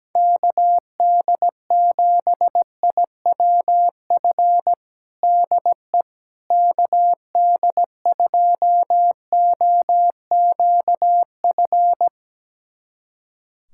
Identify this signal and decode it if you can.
Morse code, heard as a keyed tone, about 17 wpm, 700 Hz. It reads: KD7IWF DE KD2OQF
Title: USA Callsign DE Callsign - 6 Characters - 17wpm